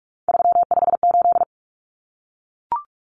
Morse code, heard as a keyed tone, about 44 wpm, 700 Hz. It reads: 358
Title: Common Cut Numbers - 3-Digit 45wpm